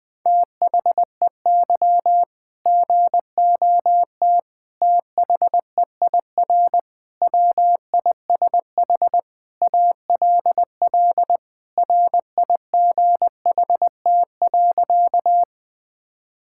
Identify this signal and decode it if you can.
Morse code, heard as a keyed tone, about 21 wpm, 700 Hz. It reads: THEY GOT THEIR WISH ALL RIGHT.
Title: Sentences from Top 1600 Words 20wpm